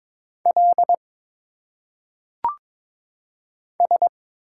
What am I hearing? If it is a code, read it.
Morse code, heard as a keyed tone, about 22 wpm, 700 Hz. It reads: L S
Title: Single Letters - Phonetic Pronunciation 22wpm